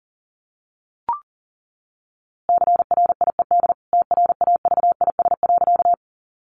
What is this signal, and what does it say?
Morse code, heard as a keyed tone, about 37 wpm, 700 Hz. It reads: CRIED TRAVIS.
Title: A Sound Of Thunder 40wpm